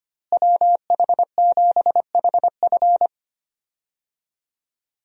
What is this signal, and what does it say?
Morse code, heard as a keyed tone, about 25 wpm, 700 Hz. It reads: WH7HF
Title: USA Call Signs - 5 Characters - Repeated 3x - 25wpm